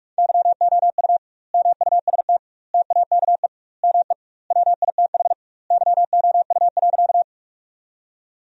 Morse code, heard as a keyed tone, about 41 wpm, 700 Hz. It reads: YOU MUST TAKE ME WITH YOU.
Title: 2x Word-Spacing - Sentences from Top 200 Words - 45wpm